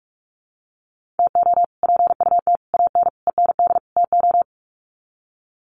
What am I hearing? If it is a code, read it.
Morse code, heard as a keyed tone, about 44 wpm, 690 Hz. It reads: TO PUT AN END TO
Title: Sets of 5 Words Encore2 45wpm